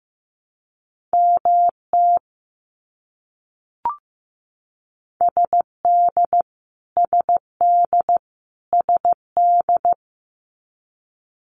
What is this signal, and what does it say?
Morse code, heard as a keyed tone, about 15 wpm, 700 Hz. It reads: MT SD SD SD
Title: US States Repeated 3x 15wpm